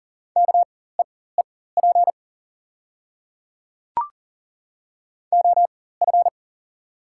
Morse code, heard as a keyed tone, about 40 wpm, spaced as for 19 wpm, 700 Hz. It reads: KEEP OF
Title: 4x Character Spacing - Top 300 Words - 40wpm